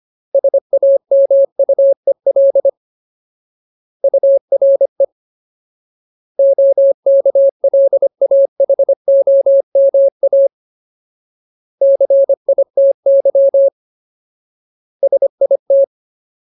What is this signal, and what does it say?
Morse code, heard as a keyed tone, about 25 wpm, 550 Hz. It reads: SAMUEL URE OKLAHOMA CITY SIT
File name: Ultimate Word List / 25WPM Ultimate